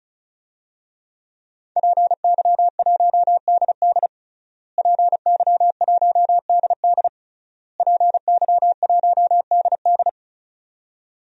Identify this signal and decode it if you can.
Morse code, heard as a keyed tone, about 36 wpm, 700 Hz. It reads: PY1DD PY1DD PY1DD
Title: DXCC-100 Call Signs - 5 Characters - Repeated 3x - 35wpm